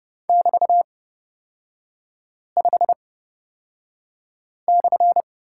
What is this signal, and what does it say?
Morse code, heard as a keyed tone, about 30 wpm, spaced as for 5 wpm, 700 Hz. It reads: <BT>5/
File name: Lesson 36e - BT - Three Characters 30wpm